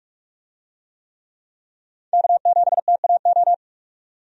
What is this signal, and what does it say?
Morse code, heard as a keyed tone, about 47 wpm, 700 Hz. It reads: K7TAO
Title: USA Call Signs - 5 Characters - 45wpm